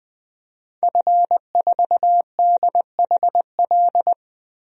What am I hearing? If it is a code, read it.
Morse code, heard as a keyed tone, about 20 wpm, 700 Hz. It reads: F4DHL
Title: DXCC-100 Call Signs - 5 Characters - 20wpm